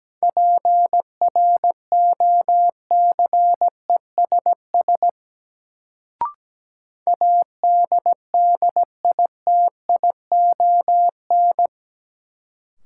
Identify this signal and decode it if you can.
Morse code, heard as a keyed tone, about 17 wpm, 700 Hz. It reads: PROCESS ADDITION
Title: Sets of 3 Words - New Word Drill - 17wpm